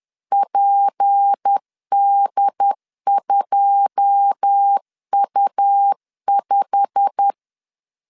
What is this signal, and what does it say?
Morse code, heard as a keyed tone, about 10 wpm, 800 Hz. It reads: PD2U5
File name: LEZIONI / 9 2 10wpm 2736